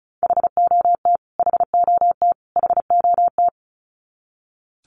Morse code, heard as a keyed tone, about 34 wpm, 700 Hz. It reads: HOT HOT HOT
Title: Top 300 Words Repeated 35wpm